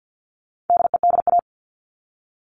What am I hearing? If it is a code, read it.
Morse code, heard as a keyed tone, about 48 wpm, 700 Hz. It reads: DEDA
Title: All Cut Numbers - 4-Digit 50wpm